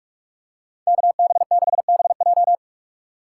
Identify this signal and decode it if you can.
Morse code, heard as a keyed tone, about 46 wpm, 700 Hz. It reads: KB6BJ
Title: USA Call Signs - 5 Characters - 6x Character Spacing - 45wpm